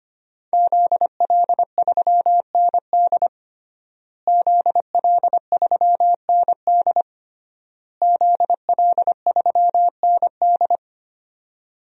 Morse code, heard as a keyed tone, about 25 wpm, 700 Hz. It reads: ZL3ND ZL3ND ZL3ND
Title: DXCC-100 Call Signs - 5 Characters - Repeated 3x - 25wpm